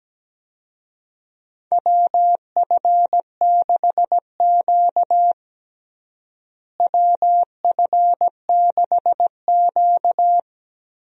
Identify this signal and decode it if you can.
Morse code, heard as a keyed tone, about 17 wpm, 700 Hz. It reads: WF6Q WF6Q
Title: USA Call Signs - 4 Characters - Repeated 3x - 17wpm